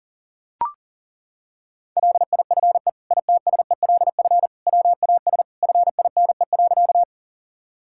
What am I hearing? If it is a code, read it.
Morse code, heard as a keyed tone, about 40 wpm, 700 Hz. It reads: LIFE ITSELF WAS FINE.
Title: Sentences from Top 1000 Words - Review Part 5 - 40wpm